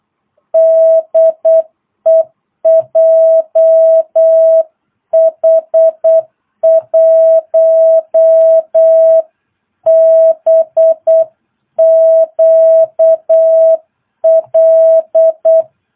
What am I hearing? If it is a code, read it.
Morse code, heard as a keyed tone, about 8 wpm, 650 Hz. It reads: DE JH1BQL
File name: jh8cbh / 2022cw